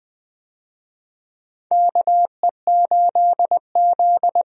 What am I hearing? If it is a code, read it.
Morse code, heard as a keyed tone, about 20 wpm, 700 Hz. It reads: KE8Z
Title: USA Call Signs - 4 Characters - 20wpm